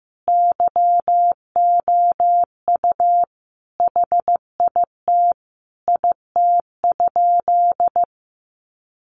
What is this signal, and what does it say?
Morse code, heard as a keyed tone, about 15 wpm, 700 Hz. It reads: YOU HIT IT?
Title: Sentences from Top 1200 Words 15wpm